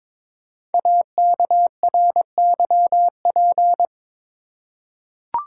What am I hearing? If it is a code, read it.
Morse code, heard as a keyed tone, about 22 wpm, 700 Hz. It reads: AKRYP
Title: Code Groups - 5 Letters - 22wpm